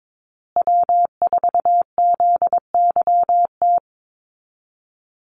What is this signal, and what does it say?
Morse code, heard as a keyed tone, about 22 wpm, 690 Hz. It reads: W4ZYT
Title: Call Signs Repeated 3x - 22wpm